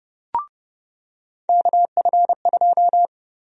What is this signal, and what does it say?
Morse code, heard as a keyed tone, about 30 wpm, 700 Hz. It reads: KF2
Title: USA Call Sign Prefixes - 30wpm